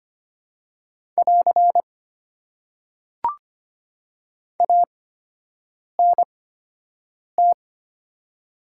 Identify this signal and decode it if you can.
Morse code, heard as a keyed tone, about 25 wpm, spaced as for 7 wpm, 690 Hz. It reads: <AR> ANT
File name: Top 90 QSO Elements - 8x Character Spacing - 25wpm